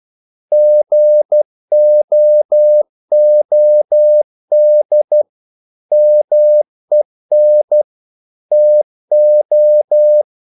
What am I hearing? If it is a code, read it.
Morse code, heard as a keyed tone, about 12 wpm, 600 Hz. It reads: GOOD MEN TO
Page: English quotations in CW (telegraphy - morse)